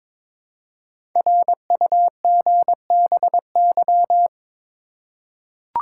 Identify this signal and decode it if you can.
Morse code, heard as a keyed tone, about 22 wpm, 700 Hz. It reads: RUGBY
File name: 5 Letter Words - First 3 Letters Given - 22wpm